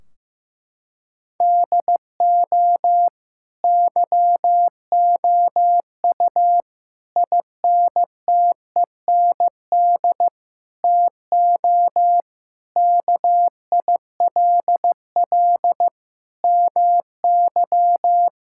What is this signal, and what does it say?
Morse code, heard as a keyed tone, about 15 wpm, 700 Hz. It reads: DO YOU INTEND TO KILL MY
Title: Sentences from Top 1600 Words 15wpm